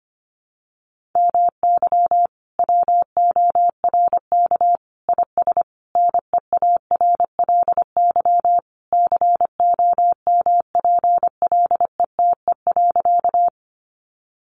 Morse code, heard as a keyed tone, about 25 wpm, 700 Hz. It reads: MY WORK IS NEARLY COMPLETE.
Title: Sentences from Top 1300 Words 25wpm